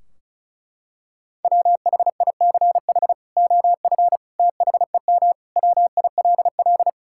Morse code, heard as a keyed tone, about 34 wpm, 700 Hz. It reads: WHICH OF THEM WILL
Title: Sentences from Top 1600 Words 35wpm